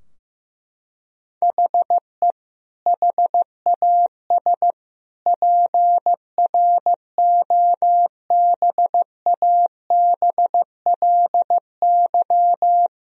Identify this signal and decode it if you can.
Morse code, heard as a keyed tone, about 16 wpm, 700 Hz. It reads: HE HAS PROBABLY
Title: Sentences from Top 1000 Words - Review Part 4 - 15wpm